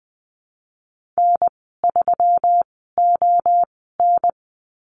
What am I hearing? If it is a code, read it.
Morse code, heard as a keyed tone, about 20 wpm, spaced as for 15 wpm, 700 Hz. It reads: N3ON
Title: 193 Lesson 31g - Letter v - Call Signs 20wpm